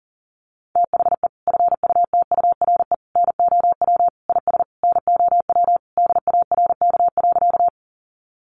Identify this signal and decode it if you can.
Morse code, heard as a keyed tone, about 40 wpm, 700 Hz. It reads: THE FUTURE NOW IS NOW DARK.
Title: Sentences from Top 700 Words 40wpm